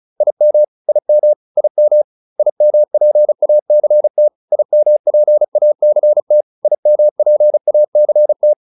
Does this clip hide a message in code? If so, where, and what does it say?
Yes: Morse code from 0.2 to 8.5 s, heard as a keyed tone, about 34 wpm, 600 Hz. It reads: IM IM IM IMPACT IMPACT IMPACT